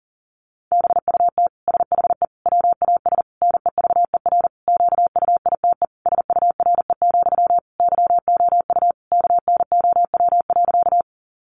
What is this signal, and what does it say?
Morse code, heard as a keyed tone, about 40 wpm, 700 Hz. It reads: BUT SHE WAS NEVER QUITE SURE, YOU KNOW.
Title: Sentences from Top 700 Words 40wpm